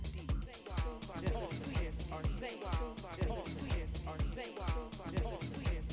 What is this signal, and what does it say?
Speech. There is no code.